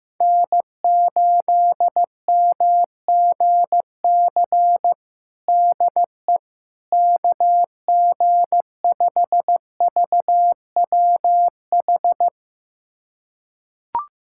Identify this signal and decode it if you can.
Morse code, heard as a keyed tone, about 15 wpm, 700 Hz. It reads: N8MGC DE KG5VWH
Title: USA Callsign DE Callsign 15wpm